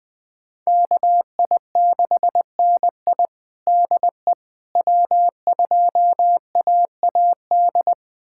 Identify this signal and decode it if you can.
Morse code, heard as a keyed tone, about 20 wpm, 700 Hz. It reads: KI6NI DE W2AAD
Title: USA Callsign DE Callsign - 5 Characters - 20wpm